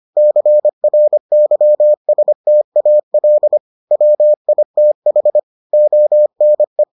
Morse code, heard as a keyed tone, about 25 wpm, 600 Hz. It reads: CRYSTAL WITH ONE